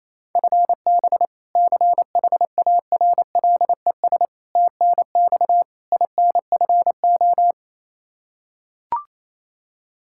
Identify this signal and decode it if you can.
Morse code, heard as a keyed tone, about 28 wpm, 700 Hz. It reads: FB CHARLES TNX INFO
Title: Practice QSO - Exchange 3 - More Info - 28wpm